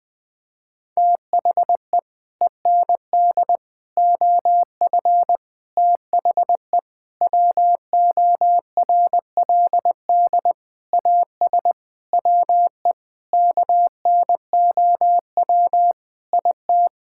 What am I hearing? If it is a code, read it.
Morse code, heard as a keyed tone, about 20 wpm, 700 Hz. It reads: THE END OF THE WORLD AS WE KNOW IT
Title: Sets of 9 Words - 20wpm